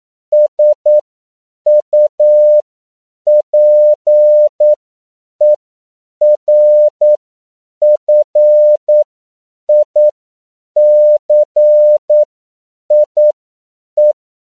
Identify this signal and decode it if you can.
Morse code, heard as a keyed tone, about 9 wpm, spaced as for 7 wpm, 600 Hz. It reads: SUPERFICIE
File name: CW / 10PPM